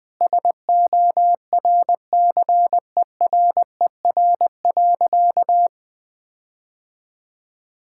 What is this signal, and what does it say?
Morse code, heard as a keyed tone, about 21 wpm, 700 Hz. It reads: SORCERER.
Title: All-Age Appropriate Jokes - 20wpm